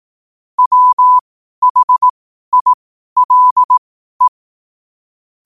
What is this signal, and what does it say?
Morse code, heard as a keyed tone, about 18 wpm, spaced as for 12 wpm, 1000 Hz. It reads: WHILE
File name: qotdfiles / 12wpm